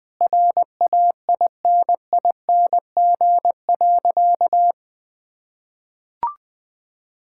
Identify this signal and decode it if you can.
Morse code, heard as a keyed tone, about 20 wpm, 700 Hz. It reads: RAINING.
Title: Sentences with 5-Word Phrases - Spoken Beginning - 20wpm